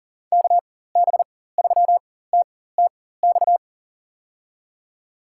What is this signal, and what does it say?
Morse code, heard as a keyed tone, about 37 wpm, spaced as for 18 wpm, 700 Hz. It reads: KB3TTX
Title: USA Call Signs - 6 Characters - 4x Character Spacing - 40wpm